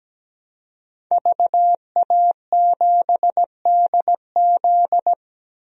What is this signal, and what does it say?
Morse code, heard as a keyed tone, about 17 wpm, 700 Hz. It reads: VA7DZ
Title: DXCC-100 Call Signs - 5 Characters - 17wpm